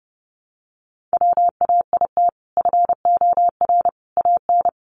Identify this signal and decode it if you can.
Morse code, heard as a keyed tone, about 30 wpm, 700 Hz. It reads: WAIT FOR AN
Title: Sentences from Top 1000 Words 30wpm